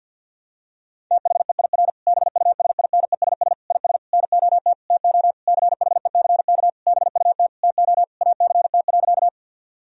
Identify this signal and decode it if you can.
Morse code, heard as a keyed tone, about 50 wpm, 700 Hz. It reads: THEIR BUSINESS IS NOT TO CHECK BUT TO ACT.
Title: Sentences from Top 1000 Words - Review Part 3 - 50wpm